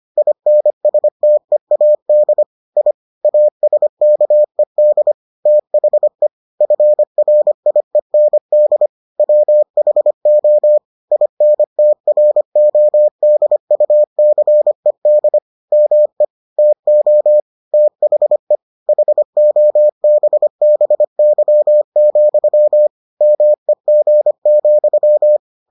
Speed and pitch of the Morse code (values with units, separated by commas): 25 wpm, 600 Hz